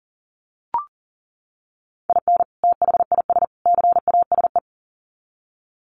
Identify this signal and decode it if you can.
Morse code, heard as a keyed tone, about 40 wpm, 700 Hz. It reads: IN THIS CASE
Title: Sets of 3 Words 40wpm